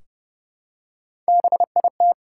Morse code, heard as a keyed tone, about 30 wpm, 700 Hz. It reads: BIT
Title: Three Letter Words - 30wpm